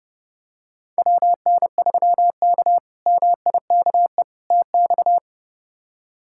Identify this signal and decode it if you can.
Morse code, heard as a keyed tone, about 30 wpm, 700 Hz. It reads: WN3K MIKE TX
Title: K1USN SST QSO Practice - 30wpm